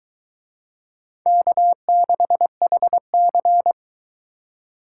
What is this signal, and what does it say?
Morse code, heard as a keyed tone, about 22 wpm, 700 Hz. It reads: K6HC